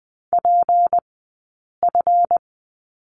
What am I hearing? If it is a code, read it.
Morse code, heard as a keyed tone, about 20 wpm, spaced as for 8 wpm, 700 Hz. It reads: PF
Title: 151 Lesson 25d - Letter p - Two Characters 20wpm